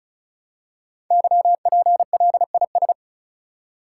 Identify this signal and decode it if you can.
Morse code, heard as a keyed tone, about 36 wpm, 700 Hz. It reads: YPLIS
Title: Code Groups - 5 Letters - 35wpm